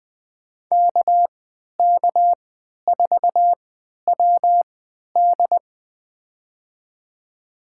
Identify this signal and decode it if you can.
Morse code, heard as a keyed tone, about 20 wpm, spaced as for 11 wpm, 700 Hz. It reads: KK4WD
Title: USA Call Signs - 5 Characters - 3x Character Spacing - 20wpm